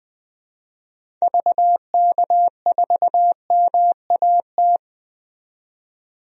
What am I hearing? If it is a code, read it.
Morse code, heard as a keyed tone, about 20 wpm, 700 Hz. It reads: VK4MAT
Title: DXCC-100 Call Signs - 6 Characters - 20wpm